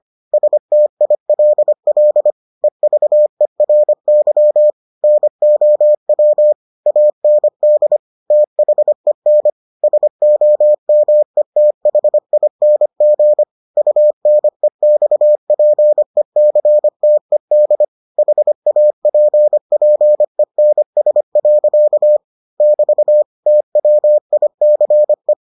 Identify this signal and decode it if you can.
Morse code, heard as a keyed tone, about 25 wpm, 600 Hz. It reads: STILL EVERY NOW AND THEN SOMETHING UNEXPECTED HAPPENS. <BT> TWICE